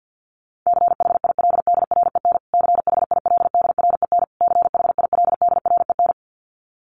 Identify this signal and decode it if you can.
Morse code, heard as a keyed tone, about 50 wpm, 700 Hz. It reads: CHILDREN CHILDREN CHILDREN
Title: Top 300 Words Repeated 50wpm